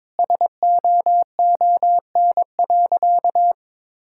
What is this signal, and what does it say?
Morse code, heard as a keyed tone, about 22 wpm, 700 Hz. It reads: SOON.
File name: practice / 22wpm / Sentences with 2-Word Phrases - 22wpm